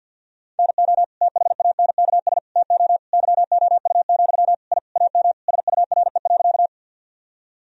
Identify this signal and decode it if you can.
Morse code, heard as a keyed tone, about 50 wpm, 700 Hz. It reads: NO THANKS TO YOU, I AM SURE.
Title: Sentences from Top 1000 Words - Review Part 3 - 50wpm